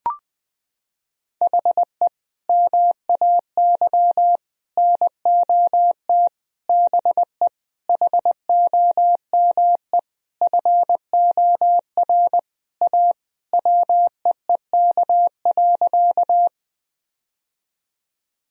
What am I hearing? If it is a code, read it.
Morse code, heard as a keyed tone, about 21 wpm, 700 Hz. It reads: HE MAY NOT BE HOME FOR A WEEK.
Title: Sentences from Top 1000 Words - Review Part 2 - 20wpm